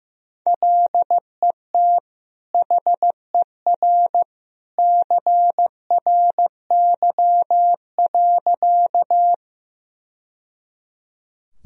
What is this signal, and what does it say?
Morse code, heard as a keyed tone, about 15 wpm, 700 Hz. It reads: LET HER CRY.
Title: Sentences from Top 1000 Words - Review Part 2 - 15wpm